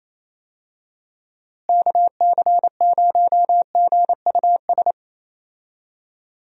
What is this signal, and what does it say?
Morse code, heard as a keyed tone, about 28 wpm, 700 Hz. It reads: KC0GUS
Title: USA Call Signs - 6 Characters - Repeated 2x - 28wpm